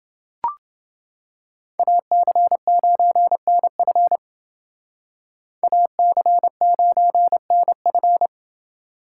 Morse code, heard as a keyed tone, about 30 wpm, 700 Hz. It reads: AC9NF AC9NF